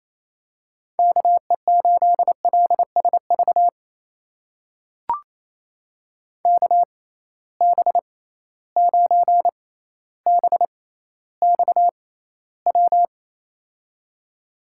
Morse code, heard as a keyed tone, about 28 wpm, spaced as for 14 wpm, 700 Hz. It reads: KE8LSV KB9BXW
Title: USA Call Signs - 6 Characters - 6x Character Spacing - 28wpm